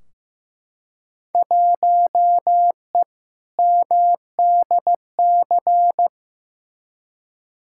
Morse code, heard as a keyed tone, about 15 wpm, 700 Hz. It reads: 1E MDC
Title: Field Day Exchanges - 15wpm